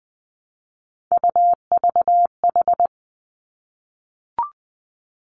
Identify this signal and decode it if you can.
Morse code, heard as a keyed tone, about 20 wpm, 700 Hz. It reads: UVH
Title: Three Challenging Letter-Number Combinations 20wpm